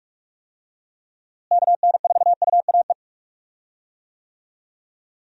Morse code, heard as a keyed tone, about 46 wpm, 700 Hz. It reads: KN4UAE
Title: USA Call Signs - 6 Characters - Repeated 3x - 45wpm